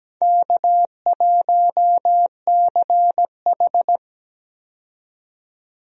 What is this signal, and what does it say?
Morse code, heard as a keyed tone, about 17 wpm, 700 Hz. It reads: K1CH
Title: USA Call Signs - 4 Characters - 17wpm